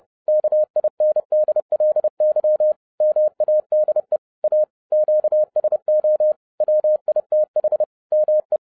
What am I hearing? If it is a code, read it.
Morse code, heard as a keyed tone, about 30 wpm, 610 Hz. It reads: KINDLY MADE A QSO WITH ME
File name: 2011-07-mp3 / n40 EXCITING QRP CW OPERATION 30wpm